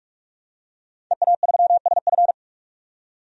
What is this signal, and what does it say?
Morse code, heard as a keyed tone, about 43 wpm, 700 Hz. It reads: EA3SF
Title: DXCC-100 Call Signs - 5 Characters - 45wpm